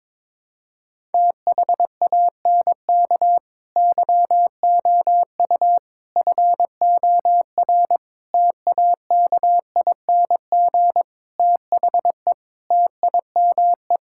Morse code, heard as a keyed tone, about 22 wpm, 700 Hz. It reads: THANK YOU FOR TAKING THE TIME
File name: practice / 22wpm / Sets of 7 Words - 22wpm